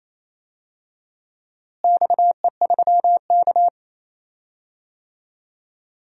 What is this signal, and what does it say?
Morse code, heard as a keyed tone, about 28 wpm, 700 Hz. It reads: XE3K
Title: DXCC-100 Call Signs - 4 Characters - Repeated 2x - 28wpm